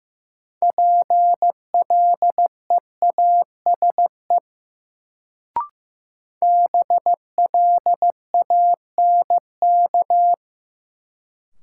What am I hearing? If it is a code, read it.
Morse code, heard as a keyed tone, about 15 wpm, 700 Hz. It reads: PLEASE BLANK
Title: Sentences from Top 1500 Words - New Word Drill Alt - 15wpm